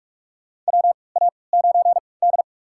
Morse code, heard as a keyed tone, about 46 wpm, spaced as for 25 wpm, 690 Hz. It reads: WA9D
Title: USA Call Signs - 4 Characters - 3x Character Spacing - 45wpm